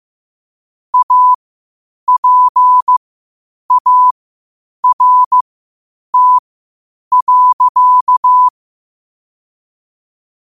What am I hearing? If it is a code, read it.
Morse code, heard as a keyed tone, about 15 wpm, spaced as for 8 wpm, 1000 Hz. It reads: APART.
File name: qotdfiles / 8wpm